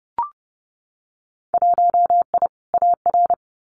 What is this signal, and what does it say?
Morse code, heard as a keyed tone, about 30 wpm, 700 Hz. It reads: 1I AR
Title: Winter Field Day Exchanges Encore 30wpm